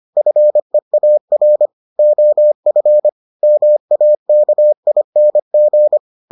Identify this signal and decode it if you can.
Morse code, heard as a keyed tone, about 25 wpm, 600 Hz. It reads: FEAR OF MAKING